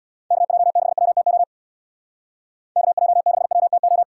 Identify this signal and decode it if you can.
Morse code, heard as a keyed tone, about 62 wpm, 700 Hz. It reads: KJ6WTO KJ6WTO
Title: Speed Racing - USA Call Signs 50wpm